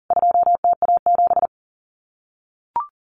Morse code, heard as a keyed tone, about 40 wpm, 700 Hz. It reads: 2TA7
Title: CWT Exchange 40wpm